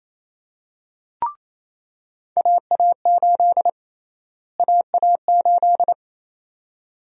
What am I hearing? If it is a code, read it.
Morse code, heard as a keyed tone, about 28 wpm, 700 Hz. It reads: AA8 AA8
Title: USA Call Sign Prefixes - Repeated 2x - 28wpm